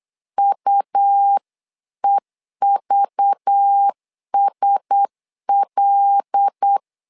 Morse code, heard as a keyed tone, about 8 wpm, 800 Hz. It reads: UEVSL
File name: LEZIONI / 5 LBVC / 5 LBV 8wpm 2837